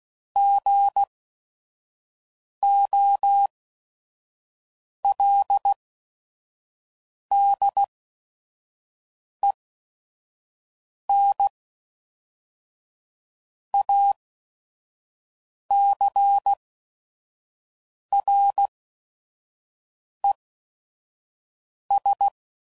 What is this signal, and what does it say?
Morse code, heard as a keyed tone, about 16 wpm, spaced as for 5 wpm, 800 Hz. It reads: GOLDEN ACRES
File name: CW Code Groups @5WPM / fromSM's / Cities